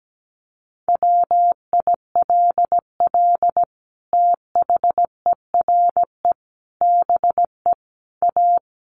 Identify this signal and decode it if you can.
Morse code, heard as a keyed tone, about 17 wpm, 700 Hz. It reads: WILL THERE BE A
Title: Sentences from Top 1300 Words 17wpm